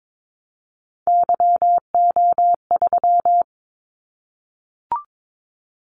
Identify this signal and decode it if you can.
Morse code, heard as a keyed tone, about 22 wpm, 700 Hz. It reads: YO3
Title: Callsign Prefixes for Top 25 DXCC - 22wpm